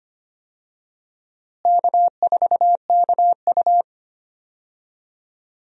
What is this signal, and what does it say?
Morse code, heard as a keyed tone, about 25 wpm, 700 Hz. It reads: K4KU
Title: USA Call Signs - 4 Characters - Repeated 3x - 25wpm